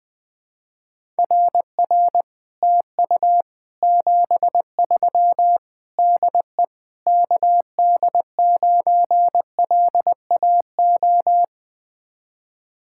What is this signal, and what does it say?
Morse code, heard as a keyed tone, about 20 wpm, 700 Hz. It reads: RR TU 73 DE KD9LAO
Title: Speed-Racing Common QSO Phrases - 20wpm